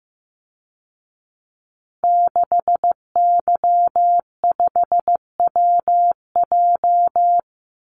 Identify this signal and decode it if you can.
Morse code, heard as a keyed tone, about 15 wpm, 700 Hz. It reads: 6Y5WJ
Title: Call Signs Repeated 3x - 15wpm